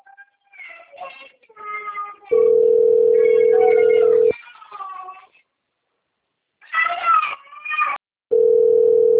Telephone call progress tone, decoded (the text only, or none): ringback tone